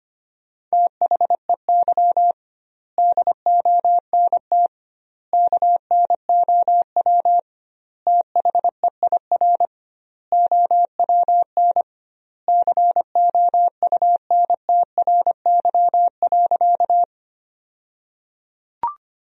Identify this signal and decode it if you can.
Morse code, heard as a keyed tone, about 25 wpm, 700 Hz. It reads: THEY DONT KNOW THEIR OWN COUNTRY.
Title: 2x Word-Spacing - Sentences from Top 300 Words - 25wpm